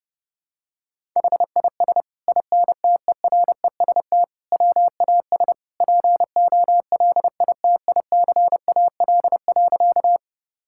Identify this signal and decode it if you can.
Morse code, heard as a keyed tone, about 30 wpm, 700 Hz. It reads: HIS INTEREST WAS POLITICAL.
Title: Sentences from Top 1000 Words - Review Part 5 - 30wpm